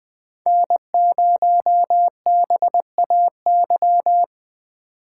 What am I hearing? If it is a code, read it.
Morse code, heard as a keyed tone, about 20 wpm, 700 Hz. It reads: N0BAY
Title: USA Call Signs - 5 Characters - 3x Character Spacing - 20wpm